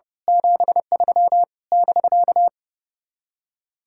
Morse code, heard as a keyed tone, about 29 wpm, 700 Hz. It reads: 73 <BK>
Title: Speed-Racing Common QSO Phrases - 30wpm